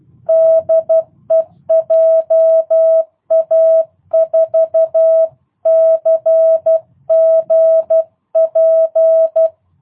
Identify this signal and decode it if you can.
Morse code, heard as a keyed tone, about 12 wpm, 650 Hz. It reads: DE JA4CGP
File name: jh8cbh / 2022cw